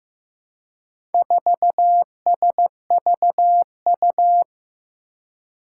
Morse code, heard as a keyed tone, about 15 wpm, 700 Hz. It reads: 4SVU
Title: Four Challenging Letter-Number Combinations 15wpm